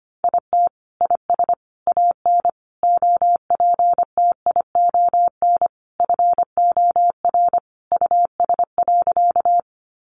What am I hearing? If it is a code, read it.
Morse code, heard as a keyed tone, about 25 wpm, 700 Hz. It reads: IT IS AN OPTION FOR US.
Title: Sentences from Top 1300 Words 25wpm